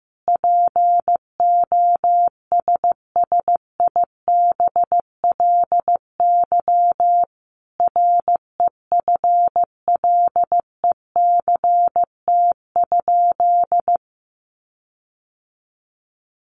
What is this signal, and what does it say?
Morse code, heard as a keyed tone, about 15 wpm, 700 Hz. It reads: POSSIBLY REFLECT?
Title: Sentences from Top 1500 Words 15wpm